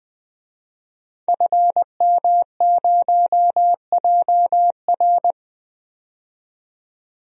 Morse code, heard as a keyed tone, about 20 wpm, 700 Hz. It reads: FM0JR
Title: DXCC-100 Call Signs - 5 Characters - 20wpm